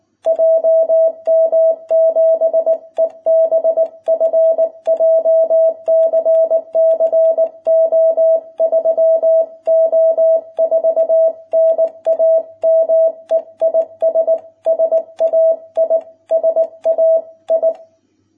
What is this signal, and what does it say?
Morse code, heard as a keyed tone, about 19 wpm, 660 Hz. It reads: JM7EBF JCC O3O4 NAME IS SAI SAI